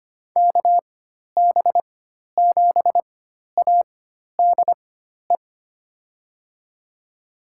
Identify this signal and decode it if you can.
Morse code, heard as a keyed tone, about 25 wpm, spaced as for 12 wpm, 700 Hz. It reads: KB7ADE
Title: USA Call Signs - 4x Character Spacing - 25wpm